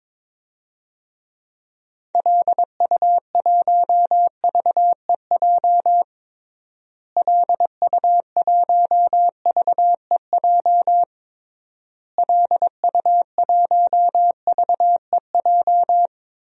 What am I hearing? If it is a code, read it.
Morse code, heard as a keyed tone, about 22 wpm, 700 Hz. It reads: LU1VEJ LU1VEJ LU1VEJ
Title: DXCC-100 Call Signs - 6 Characters - Repeated 3x - 22wpm